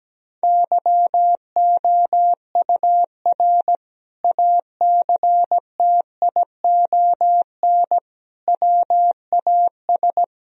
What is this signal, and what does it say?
Morse code, heard as a keyed tone, about 17 wpm, 700 Hz. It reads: YOUR ACTION WAS
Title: Sentences from Top 1000 Words - Review Part 4 - 17wpm